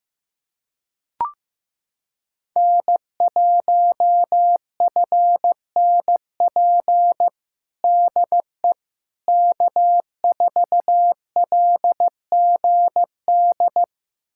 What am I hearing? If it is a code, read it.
Morse code, heard as a keyed tone, about 15 wpm, 700 Hz. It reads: N1FNP DE K4LGD
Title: USA Callsign DE Callsign - 5 Characters - 15wpm